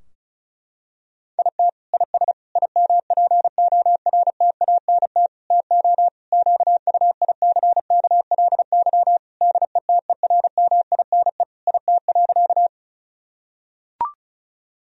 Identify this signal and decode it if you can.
Morse code, heard as a keyed tone, about 34 wpm, 700 Hz. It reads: IT IS IMPORTANT TO QUICKLY DETERMINE IT.
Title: Sentences from Top 1000 Words - Review Part 4 - 35wpm